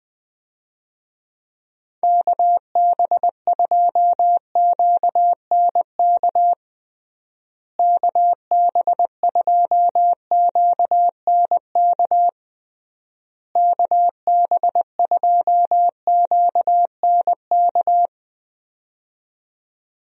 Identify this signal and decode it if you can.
Morse code, heard as a keyed tone, about 20 wpm, 700 Hz. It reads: KB2QNK KB2QNK KB2QNK
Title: USA Call Signs - 6 Characters - Repeated 3x - 20wpm